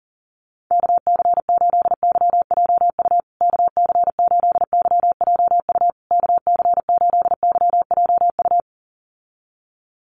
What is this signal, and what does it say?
Morse code, heard as a keyed tone, about 40 wpm, 700 Hz. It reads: KC8YJU KC8YJU KC8YJU
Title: USA Call Signs Repeated 3x 40wpm